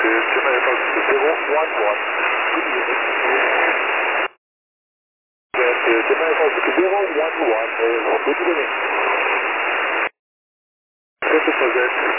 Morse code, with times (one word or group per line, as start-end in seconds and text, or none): none